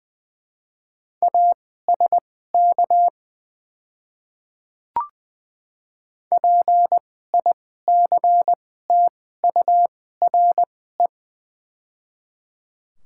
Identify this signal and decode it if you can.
Morse code, heard as a keyed tone, about 20 wpm, spaced as for 14 wpm, 700 Hz. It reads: ASK PICTURE
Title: 2x Character Spacing - Top 200 Words - 20wpm